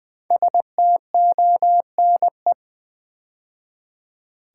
Morse code, heard as a keyed tone, about 20 wpm, 700 Hz. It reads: STONE
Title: 5 Letter Words - First 3 Letters Given - 20wpm